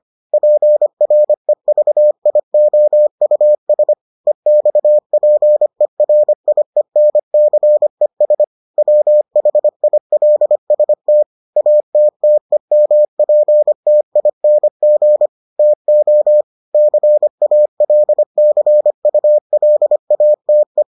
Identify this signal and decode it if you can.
Morse code, heard as a keyed tone, about 25 wpm, 600 Hz. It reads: PREVIOUS EXPERIENCES WHILST ATTEMPTING TO CALCULATE